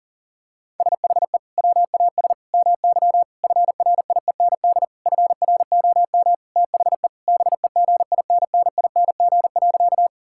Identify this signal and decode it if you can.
Morse code, heard as a keyed tone, about 40 wpm, 700 Hz. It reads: SHE WAS MY FRIEND FROM THE BEGINNING.
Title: Sentences from Top 1000 Words - Review Part 5 - 40wpm